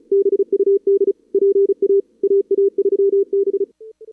Morse code, heard as a keyed tone, about 31 wpm, 440 Hz. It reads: BUD PA AA3B
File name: CWSprint / Feb2022 / k5tr-feb-na-sprint-cw-k5ot -0000